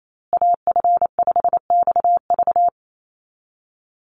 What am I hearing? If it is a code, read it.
Morse code, heard as a keyed tone, about 28 wpm, 700 Hz. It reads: AF5XV
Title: USA Call Signs Repeated 2x 28wpm